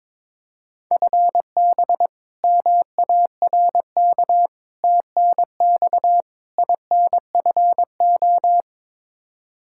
Morse code, heard as a keyed tone, about 21 wpm, 700 Hz. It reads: FB MARK TNX INFO
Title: K1USN SST QSO Practice 22wpm